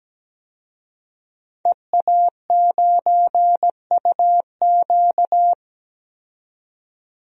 Morse code, heard as a keyed tone, about 17 wpm, 700 Hz. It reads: EA9UQ
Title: DXCC-100 Call Signs - 5 Characters - 17wpm